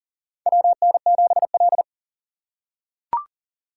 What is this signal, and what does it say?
Morse code, heard as a keyed tone, about 40 wpm, 700 Hz. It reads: WN7L
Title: USA Call Signs - 4 Characters - 2x Character Spacing - 40wpm